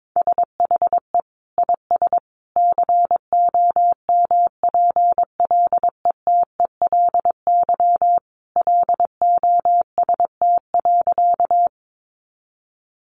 Morse code, heard as a keyed tone, about 22 wpm, 700 Hz. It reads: SHE IS COMPLETELY LOST.
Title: Additional Easy Phrases 22wpm